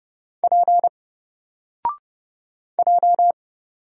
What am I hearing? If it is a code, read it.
Morse code, heard as a keyed tone, about 30 wpm, 700 Hz. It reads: P J
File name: Single-Letter-Number-30wpm